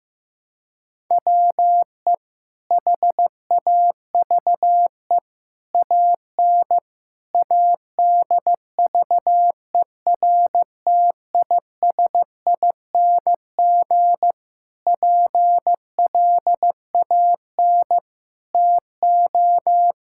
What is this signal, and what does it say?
Morse code, heard as a keyed tone, about 15 wpm, 700 Hz. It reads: WE HAVE AN ADVERTISING PLAN TO
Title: Sentences from Top 1000 Words - Review Part 6 - 15wpm